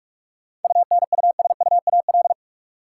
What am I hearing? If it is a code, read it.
Morse code, heard as a keyed tone, about 46 wpm, 700 Hz. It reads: UNUSUAL
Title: Sentences from Top 1400 Words - New Word Drill Alt - 45wpm